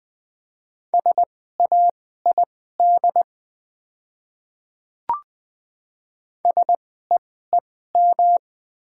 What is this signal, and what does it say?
Morse code, heard as a keyed tone, about 20 wpm, spaced as for 14 wpm, 700 Hz. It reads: SAID SEEM
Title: 2x Character Spacing - Top 500 Words - 20wpm